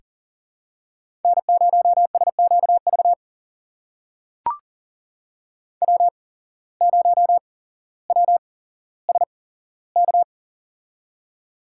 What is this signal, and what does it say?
Morse code, heard as a keyed tone, about 40 wpm, spaced as for 17 wpm, 700 Hz. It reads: N0SQV W0WSK
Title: USA Call Signs - 5 Characters - 8x Character Spacing - 40wpm